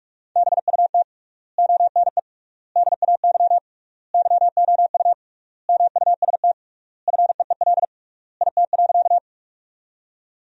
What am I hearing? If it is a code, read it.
Morse code, heard as a keyed tone, about 46 wpm, 700 Hz. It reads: BUT ONE DAY YOU MUST FEEL IT.
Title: 3x Word-Spacing - Sentences from Top 400 Words - 45wpm